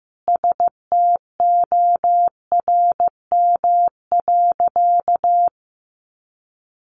Morse code, heard as a keyed tone, about 15 wpm, 700 Hz. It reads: STORM.
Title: Sentences from Top 1500 Words 15wpm